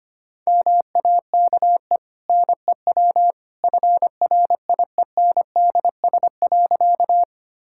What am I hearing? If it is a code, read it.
Morse code, heard as a keyed tone, about 25 wpm, 690 Hz. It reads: MAKE NEW FRIENDS.
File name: Easy Sentences with Hints - 25wpm